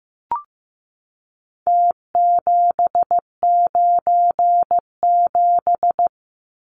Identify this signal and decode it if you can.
Morse code, heard as a keyed tone, about 15 wpm, 700 Hz. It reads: T797
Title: Numbers - Contest Style - 4-Digit 15wpm